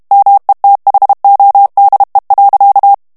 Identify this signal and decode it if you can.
Morse code, heard as a keyed tone, about 32 wpm, 800 Hz. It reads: METHODE.
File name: HOLLANDAIS texte clair 2 vitesse 30 mots minute